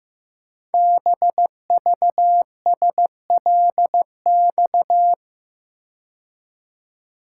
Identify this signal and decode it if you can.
Morse code, heard as a keyed tone, about 15 wpm, 700 Hz. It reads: BVSLX
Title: Code Groups - 5 Letter:Numbers - 15wpm